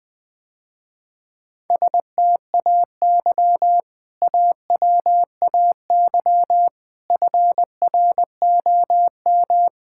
Morse code, heard as a keyed tone, about 20 wpm, 700 Hz. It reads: STAY AWAY FROM
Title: Sentences with 3-Word Phrases - Spoken Beginning - 20wpm